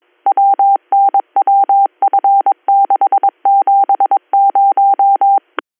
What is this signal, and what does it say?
Morse code, heard as a keyed tone, about 22 wpm, 790 Hz. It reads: WNWF670
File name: 310- .0-call 58573